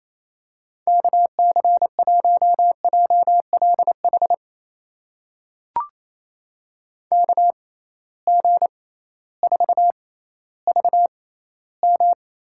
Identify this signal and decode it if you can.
Morse code, heard as a keyed tone, about 28 wpm, spaced as for 15 wpm, 700 Hz. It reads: KC1JLH KG4VM
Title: USA Call Signs - 6 Characters - 6x Character Spacing - 28wpm